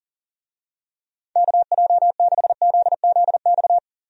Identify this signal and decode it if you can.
Morse code, heard as a keyed tone, about 40 wpm, 700 Hz. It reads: KJ6ZZX
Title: USA Call Signs - 6 Characters - Repeated 2x - 40wpm